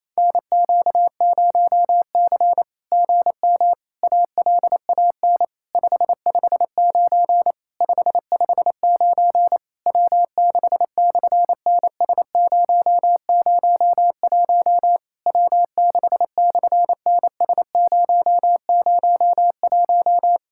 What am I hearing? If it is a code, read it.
Morse code, heard as a keyed tone, about 27 wpm, 700 Hz. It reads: NQ0C GM ALAN 559 559 W6/NS001 W6/NS001
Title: SOTA QSO Practice 28wpm